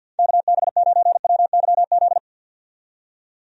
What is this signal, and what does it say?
Morse code, heard as a keyed tone, about 51 wpm, 700 Hz. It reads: KB9WYZ
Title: USA Call Signs - 6 Characters - 50wpm